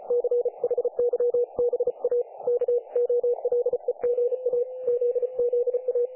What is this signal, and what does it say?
Morse code, heard as a keyed tone, about 35 wpm, 500 Hz. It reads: CHYBA KOLEGA ZZA
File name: dlazaawansowanych / nagraniazpasma / websdr recording 3564.7kHz